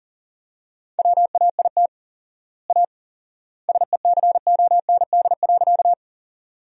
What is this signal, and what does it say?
Morse code, heard as a keyed tone, about 37 wpm, 700 Hz. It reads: WAIT A SECOND.
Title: 4x Word-Spacing - Sentences from Top 500 Words - 40wpm